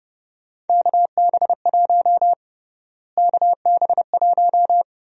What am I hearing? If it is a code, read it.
Morse code, heard as a keyed tone, about 32 wpm, 700 Hz. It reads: KB1 KB1
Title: USA Call Sign Prefixes - Repeated 2x - 30wpm